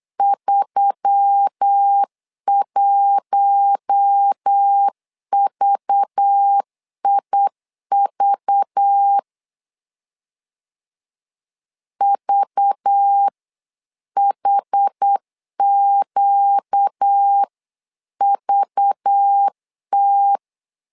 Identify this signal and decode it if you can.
Morse code, heard as a keyed tone, about 8 wpm, 800 Hz. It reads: 31VIV VHQVT
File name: LEZIONI / 5 LBVC / 5 LBV 8wpm 2837